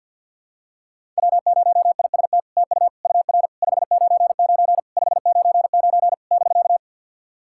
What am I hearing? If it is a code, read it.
Morse code, heard as a keyed tone, about 49 wpm, 700 Hz. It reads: W0IST TU UR 599 599 <BK>
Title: SOTA QSO Practice 50wpm